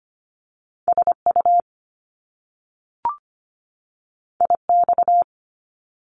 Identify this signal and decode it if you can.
Morse code, heard as a keyed tone, about 25 wpm, 700 Hz. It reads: SU IX
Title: Top Two Letter Combinations in Words 25wpm